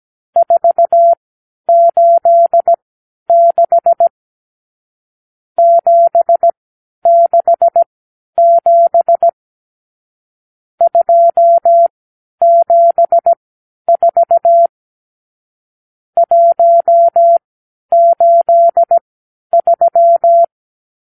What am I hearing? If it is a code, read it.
Morse code, heard as a keyed tone, about 17 wpm, spaced as for 10 wpm, 680 Hz. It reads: 486 767 274 183
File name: k7qo / Lesson 54 (9)